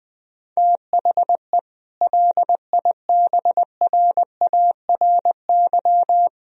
Morse code, heard as a keyed tone, about 21 wpm, 700 Hz. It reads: THE LIBRARY
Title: Sentences from Top 1000 Words - Review Part 4 - 20wpm